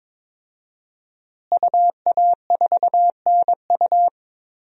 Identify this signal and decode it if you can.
Morse code, heard as a keyed tone, about 21 wpm, 700 Hz. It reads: UA4NU
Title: DXCC-100 Call Signs - 5 Characters - Repeated 3x - 22wpm